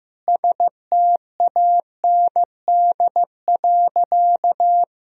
Morse code, heard as a keyed tone, about 15 wpm, 700 Hz. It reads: STAND.
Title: Sentences from Top 1000 Words - Review Part 2 - 15wpm